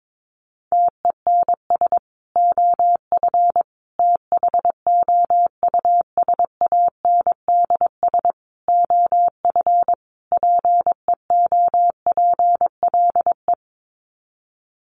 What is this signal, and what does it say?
Morse code, heard as a keyed tone, about 22 wpm, 700 Hz. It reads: TENS OF THOUSANDS OF PEOPLE
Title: Sets of 5 Words Encore2 22wpm